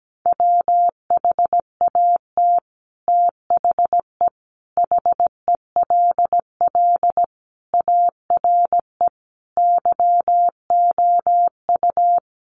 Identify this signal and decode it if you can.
Morse code, heard as a keyed tone, about 17 wpm, 700 Hz. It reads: WHAT THE HELL ARE YOU
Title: Sets of 5 Words Encore 17wpm